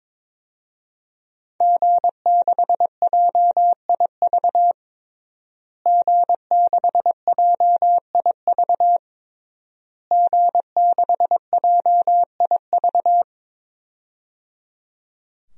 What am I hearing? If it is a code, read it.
Morse code, heard as a keyed tone, about 22 wpm, 700 Hz. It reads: G6JIV G6JIV G6JIV
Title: DXCC-100 Call Signs - 5 Characters - Repeated 3x - 22wpm